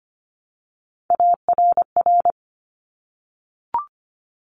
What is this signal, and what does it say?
Morse code, heard as a keyed tone, about 25 wpm, 700 Hz. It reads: ARR
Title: Top Three Letter Combinations in Words 25wpm